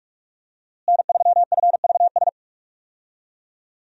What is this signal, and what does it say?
Morse code, heard as a keyed tone, about 44 wpm, 700 Hz. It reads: N3FVS
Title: USA Call Signs - 5 Characters - Repeated 2x - 45wpm